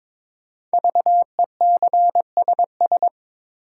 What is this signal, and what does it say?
Morse code, heard as a keyed tone, about 22 wpm, 700 Hz. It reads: VECSS
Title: Code Groups - 5 Letters - 22wpm